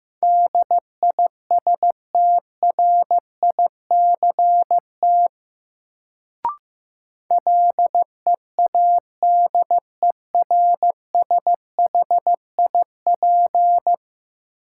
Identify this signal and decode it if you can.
Morse code, heard as a keyed tone, about 15 wpm, 700 Hz. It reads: DISTRICT LEADERSHIP
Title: Sentences from Top 1600 Words - New Word Drill - 15wpm